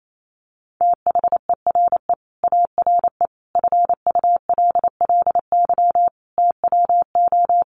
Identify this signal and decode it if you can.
Morse code, heard as a keyed tone, about 28 wpm, 700 Hz. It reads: THERE ARE FULLY TWO
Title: Sentences from Top 1300 Words 28wpm